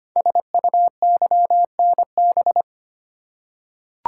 Morse code, heard as a keyed tone, about 25 wpm, 700 Hz. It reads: SUYNB
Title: Code Groups - 5 Letters - 25wpm